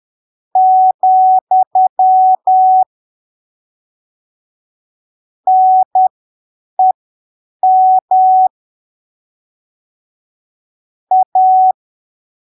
Morse code, heard as a keyed tone, about 10 wpm, spaced as for 7 wpm, 750 Hz. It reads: , NEM A
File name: arquivos / texto / texto 28 10ppm